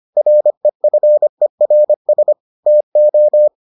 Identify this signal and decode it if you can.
Morse code, heard as a keyed tone, about 25 wpm, 600 Hz. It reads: REFERS TO